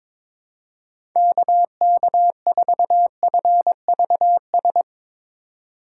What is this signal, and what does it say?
Morse code, heard as a keyed tone, about 21 wpm, 700 Hz. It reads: KK4FVS
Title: USA Call Signs - 6 Characters - 6x Character Spacing - 22wpm